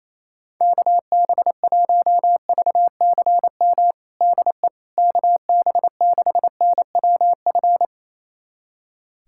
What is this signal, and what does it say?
Morse code, heard as a keyed tone, about 28 wpm, 700 Hz. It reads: KB1VCM DE KB6NWF
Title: USA Callsign DE Callsign - 6 Characters - 28wpm